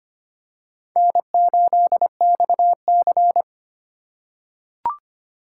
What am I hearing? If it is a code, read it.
Morse code, heard as a keyed tone, about 25 wpm, 700 Hz. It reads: N8XC
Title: USA Call Signs - 4 Characters - 2x Character Spacing - 25wpm